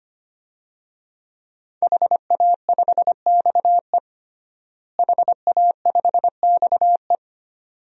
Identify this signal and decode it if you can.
Morse code, heard as a keyed tone, about 25 wpm, 700 Hz. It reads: HA5XE HA5XE
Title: DXCC-100 Call Signs - 5 Characters - Repeated 2x - 25wpm